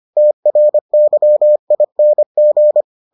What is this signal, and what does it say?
Morse code, heard as a keyed tone, about 25 wpm, 600 Hz. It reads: TRYING